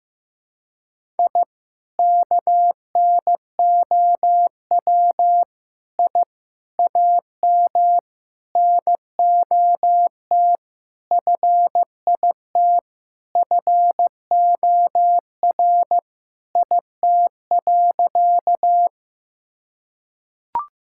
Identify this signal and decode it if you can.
Morse code, heard as a keyed tone, about 15 wpm, 700 Hz. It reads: I KNOW I AM NOT FIT FOR IT.
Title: Sentences from Top 1000 Words - Review Part 5 - 15wpm